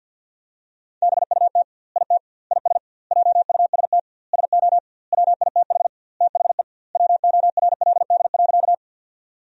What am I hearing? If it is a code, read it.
Morse code, heard as a keyed tone, about 50 wpm, 700 Hz. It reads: BUT IT IS JUST SO WITH THE WORLD.
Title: 2x Word-Spacing - Sentences from Top 200 Words - 50wpm